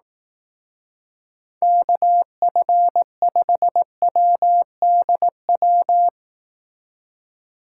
Morse code, heard as a keyed tone, about 18 wpm, 700 Hz. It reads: KF5WDW
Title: Speed Racing - USA 2x3 Call Signs 15wpm